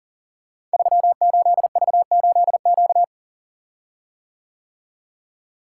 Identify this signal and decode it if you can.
Morse code, heard as a keyed tone, about 42 wpm, 700 Hz. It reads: 38V8Q
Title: Code Groups - 5 Letter:Numbers - 40wpm